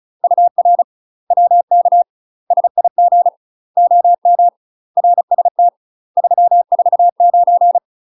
Morse code, heard as a keyed tone, about 34 wpm, 700 Hz. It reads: UR WK SIG OM RST 349
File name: UK MASTER QSOs TRAINING/35WPM